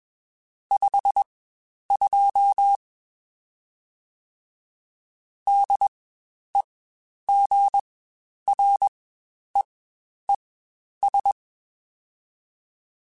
Morse code, heard as a keyed tone, about 21 wpm, spaced as for 10 wpm, 800 Hz. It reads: 52 DEGREES
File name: Spaced Characters / Spaced Characters-mp3s / Weather 4 10 sp @20